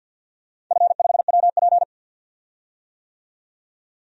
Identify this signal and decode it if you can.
Morse code, heard as a keyed tone, about 53 wpm, 700 Hz. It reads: F5WP
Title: DXCC-100 Call Signs - 4 Characters - 50wpm